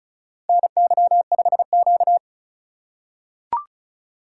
Morse code, heard as a keyed tone, about 34 wpm, 700 Hz. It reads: NY5Q
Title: USA Call Signs - 4 Characters - 4x Character Spacing - 35wpm